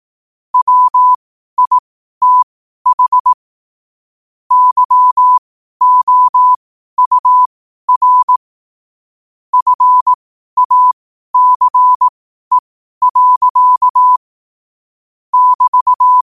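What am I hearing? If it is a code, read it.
Morse code, heard as a keyed tone, about 18 wpm, spaced as for 12 wpm, 1000 Hz. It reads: WITH YOUR FACE. <BT>
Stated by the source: 12 wpm Morse.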